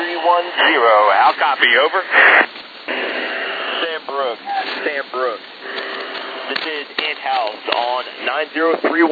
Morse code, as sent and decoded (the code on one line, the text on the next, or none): none